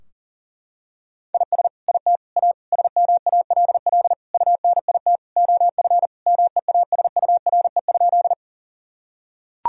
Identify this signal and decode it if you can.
Morse code, heard as a keyed tone, about 37 wpm, 700 Hz. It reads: IS IT A SMALL UNIT OF MEASURE?
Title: Sentences from Top 1000 Words - Review Part 2 - 40wpm